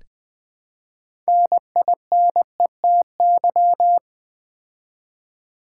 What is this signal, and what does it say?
Morse code, heard as a keyed tone, about 20 wpm, 700 Hz. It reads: NINETY
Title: Build A Word - 3 Letters At A Time - 6-Letter Words - 20wpm